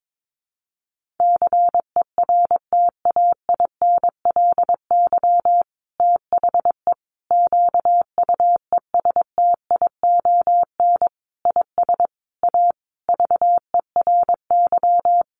Morse code, heard as a keyed tone, about 22 wpm, 700 Hz. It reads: CERTAINLY THE QUESTION IS A VERY
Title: Sentences from Top 1200 Words 22wpm